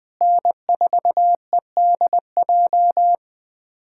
Morse code, heard as a keyed tone, about 20 wpm, 700 Hz. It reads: N4EDJ
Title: USA Callsign DE Callsign - 5 Characters - 20wpm